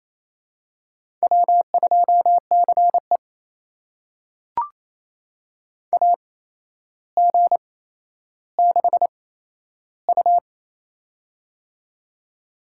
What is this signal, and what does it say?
Morse code, heard as a keyed tone, about 28 wpm, spaced as for 12 wpm, 700 Hz. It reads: W2CE AG6U
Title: USA Call Signs - 4 Characters - 8x Character Spacing - 28wpm